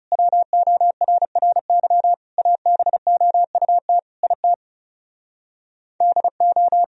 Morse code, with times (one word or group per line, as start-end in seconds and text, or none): 0.1-2.1 WORRY
2.4-4.0 ABOUT
4.2-4.5 IT
6.0-6.8 DO